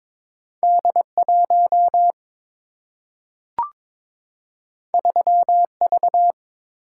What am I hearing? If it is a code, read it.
Morse code, heard as a keyed tone, about 22 wpm, 700 Hz. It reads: D 1 3 V
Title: Single-Letter Single-Number 22wpm